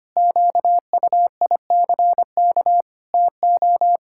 Morse code, heard as a keyed tone, about 25 wpm, 700 Hz. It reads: QUICK TO
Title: Sentences from Top 1000 Words - Review Part 6 - 25wpm